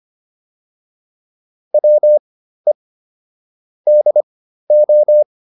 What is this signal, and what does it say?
Morse code, heard as a keyed tone, about 25 wpm, spaced as for 13 wpm, 600 Hz. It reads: WE DO